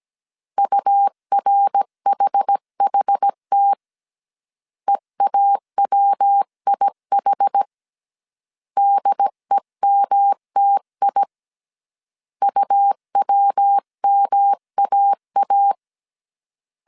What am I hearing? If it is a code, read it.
Morse code, heard as a keyed tone, about 17 wpm, 800 Hz. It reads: URHHT EAWIH DEMTI UWMAA
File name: CW / DET / 1 3 1500C S A 16wpm 2141